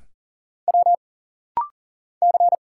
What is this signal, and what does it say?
Morse code, heard as a keyed tone, about 40 wpm, 700 Hz. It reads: W C
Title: Single Letters - Rapid-Fire 40wpm